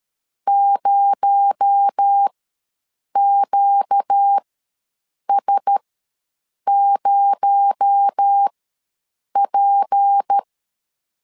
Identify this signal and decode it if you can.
Morse code, heard as a keyed tone, about 13 wpm, spaced as for 7 wpm, 800 Hz. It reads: 0QS0P
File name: LEZIONI / 6 PKX0 / 6 PKX0 ws 12wpm 3448